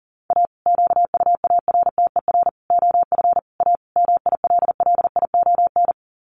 Morse code, heard as a keyed tone, about 40 wpm, 700 Hz. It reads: A QUARTER OF A MILLION
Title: Sets of 5 Words Encore2 40wpm